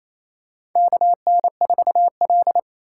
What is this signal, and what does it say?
Morse code, heard as a keyed tone, about 28 wpm, 700 Hz. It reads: KN4L
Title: USA Call Signs - 4 Characters - 3x Character Spacing - 28wpm